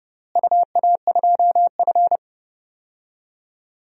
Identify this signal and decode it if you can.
Morse code, heard as a keyed tone, about 30 wpm, 700 Hz. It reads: UA2F
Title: DXCC-100 Call Signs - 4 Characters - Repeated 3x - 30wpm